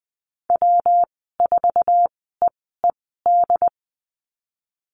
Morse code, heard as a keyed tone, about 20 wpm, spaced as for 15 wpm, 700 Hz. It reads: W4EED
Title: 207 Lesson 33g - Letter j - Call Signs 20wpm